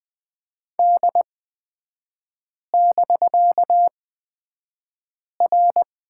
Morse code, heard as a keyed tone, about 20 wpm, spaced as for 5 wpm, 700 Hz. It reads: D<BK>R
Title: 250 Lesson 40e - BK - Three Characters 20wpm